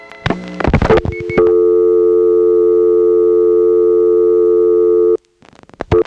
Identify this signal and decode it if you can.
Dial tone (North America), 350 + 440 Hz.